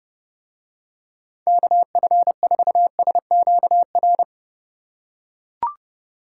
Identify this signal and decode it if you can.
Morse code, heard as a keyed tone, about 29 wpm, 700 Hz. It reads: KF4SQR